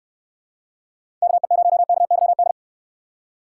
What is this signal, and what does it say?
Morse code, heard as a keyed tone, about 66 wpm, 700 Hz. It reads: KE0KYD